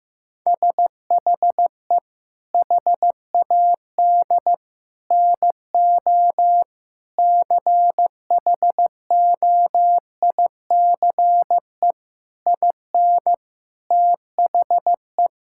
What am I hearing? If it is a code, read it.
Morse code, heard as a keyed tone, about 15 wpm, 700 Hz. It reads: SHE HAD NO CHOICE IN THE
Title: Sentences from Top 1000 Words - Review Part 6 - 15wpm